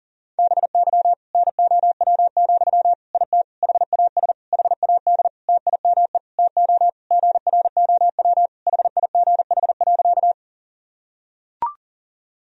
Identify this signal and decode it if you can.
Morse code, heard as a keyed tone, about 40 wpm, 700 Hz. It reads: BY NOW, IT HAS HAD TIME TO GROW HIGH.
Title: Sentences from Top 1000 Words - Review Part 1 - 40wpm